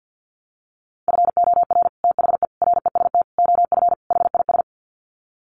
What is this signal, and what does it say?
Morse code, heard as a keyed tone, about 50 wpm, 700 Hz. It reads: FOR THE REST OF HIS
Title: Sets of 5 Words 50wpm